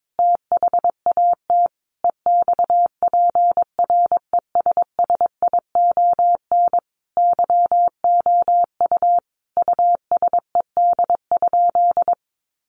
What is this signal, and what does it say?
Morse code, heard as a keyed tone, about 22 wpm, 700 Hz. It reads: THAT EXPRESSION YOU USED?
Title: Sentences from Top 1200 Words 22wpm